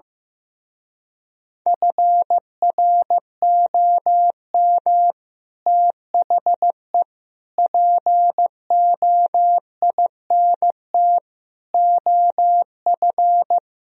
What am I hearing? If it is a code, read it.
Morse code, heard as a keyed tone, about 15 wpm, 700 Hz. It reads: FROM THE POINT OF
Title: Speed-Racing - Sets of 5 Words 15wpm